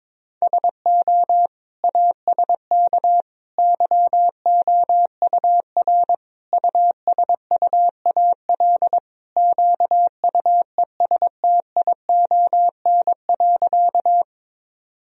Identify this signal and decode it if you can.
Morse code, heard as a keyed tone, about 22 wpm, 700 Hz. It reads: SO ASK YOUR USUAL QUESTION.
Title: Sentences from Top 1000 Words - Review Part 2 - 22wpm